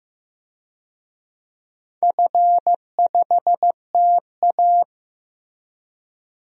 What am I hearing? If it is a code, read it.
Morse code, heard as a keyed tone, about 15 wpm, 700 Hz. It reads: F5TA
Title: DXCC-100 Call Signs - 4 Characters - Repeated 3x - 15wpm